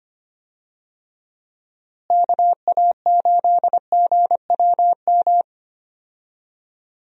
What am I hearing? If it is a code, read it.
Morse code, heard as a keyed tone, about 25 wpm, 700 Hz. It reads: KA8GWM
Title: USA Call Signs - 6 Characters - Repeated 2x - 25wpm